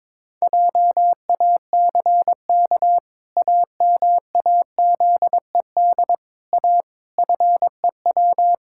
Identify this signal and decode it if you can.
Morse code, heard as a keyed tone, about 22 wpm, 700 Hz. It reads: JACK AMAZED A FEW
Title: Pangram Challenge v2 - 22wpm